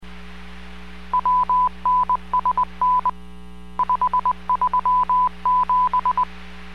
Morse code, heard as a keyed tone, about 20 wpm, spaced as for 17 wpm, 1000 Hz. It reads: WNSN537